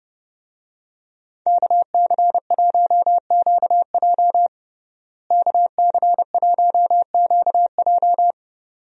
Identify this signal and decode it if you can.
Morse code, heard as a keyed tone, about 30 wpm, 700 Hz. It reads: KC1QJ KC1QJ
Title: USA Call Signs - 5 Characters - Repeated 2x - 30wpm